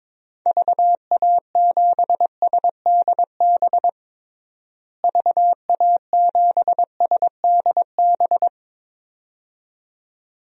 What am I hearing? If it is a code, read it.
Morse code, heard as a keyed tone, about 22 wpm, 700 Hz. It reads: VA7SDB VA7SDB
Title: DXCC-100 Call Signs - 6 Characters - Repeated 2x - 22wpm